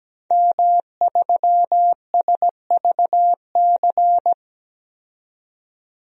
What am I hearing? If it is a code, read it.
Morse code, heard as a keyed tone, about 17 wpm, 700 Hz. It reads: M3SVC
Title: DXCC-100 Call Signs - 5 Characters - 17wpm